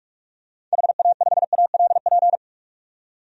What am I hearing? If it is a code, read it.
Morse code, heard as a keyed tone, about 47 wpm, 700 Hz. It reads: HA5ALP
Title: DXCC-100 Call Signs - 6 Characters - 45wpm